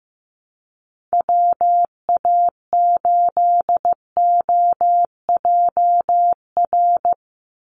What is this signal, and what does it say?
Morse code, heard as a keyed tone, about 15 wpm, 700 Hz. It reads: WA8OJR
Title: Call Signs Repeated 2x - 15wpm